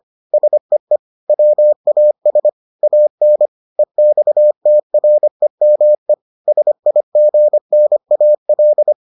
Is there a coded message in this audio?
Yes: Morse code from 0.3 to 8.9 s, heard as a keyed tone, about 25 wpm, 600 Hz. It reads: SEE WAS AN EXTREME SIGNAL